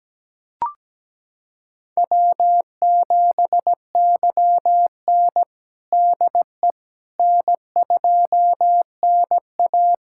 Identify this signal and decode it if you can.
Morse code, heard as a keyed tone, about 17 wpm, 700 Hz. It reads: W7YN DE N2NA
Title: USA Callsign DE Callsign - 4 Characters - 17wpm